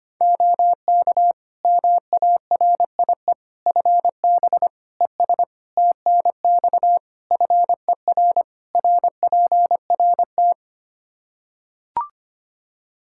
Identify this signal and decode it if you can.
Morse code, heard as a keyed tone, about 25 wpm, 700 Hz. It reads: OK MARIE FB ES TNX FER RPRT
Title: Practice QSO - Exchange 2 - Info - 25wpm